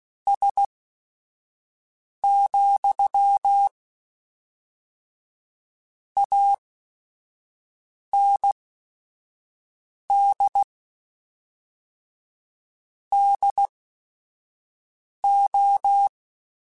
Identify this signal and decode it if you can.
Morse code, heard as a keyed tone, about 16 wpm, spaced as for 5 wpm, 800 Hz. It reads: S, AND DO
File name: Technical Subjects / Technical Subjects-mp3s / AND Gate @5